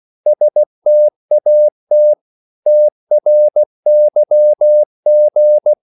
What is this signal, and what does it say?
Morse code, heard as a keyed tone, about 16 wpm, 600 Hz. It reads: STAT TRYG
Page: Korte danske ord 16wpm